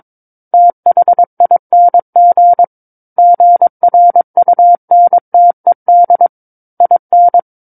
Morse code, heard as a keyed tone, about 22 wpm, 700 Hz. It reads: THING GRUNTED IN